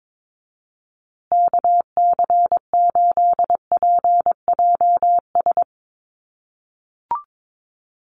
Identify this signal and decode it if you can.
Morse code, heard as a keyed tone, about 22 wpm, 700 Hz. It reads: KC8PJS
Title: USA Call Signs Repeated 3x 22wpm